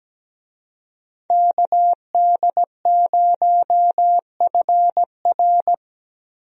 Morse code, heard as a keyed tone, about 17 wpm, 700 Hz. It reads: KD0FR
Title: USA Call Signs - 5 Characters - Repeated 3x - 17wpm